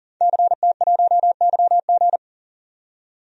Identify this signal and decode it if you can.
Morse code, heard as a keyed tone, about 40 wpm, 700 Hz. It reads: CT1YG
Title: DXCC-100 Call Signs - 5 Characters - 40wpm